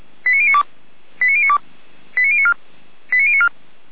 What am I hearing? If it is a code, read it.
Five-tone selective call (ZVEI), heard as quick runs of five tones, tones of about 70 ms. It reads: 89092 89092 89094 89094